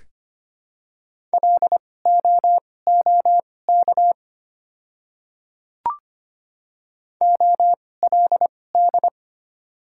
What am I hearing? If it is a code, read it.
Morse code, heard as a keyed tone, about 25 wpm, spaced as for 18 wpm, 700 Hz. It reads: LOOK OLD
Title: 2x Character Spacing - Top 200 Words - 25wpm